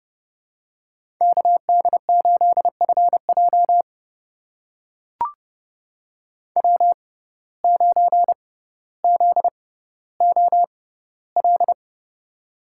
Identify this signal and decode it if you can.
Morse code, heard as a keyed tone, about 30 wpm, spaced as for 15 wpm, 700 Hz. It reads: KD8FJ W9ZOL
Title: USA Call Signs - 5 Characters - 6x Character Spacing - 30wpm